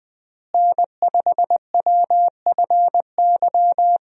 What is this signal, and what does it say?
Morse code, heard as a keyed tone, about 20 wpm, 700 Hz. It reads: N5WFY
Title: USA Call Signs - 5 Characters - Repeated 3x - 20wpm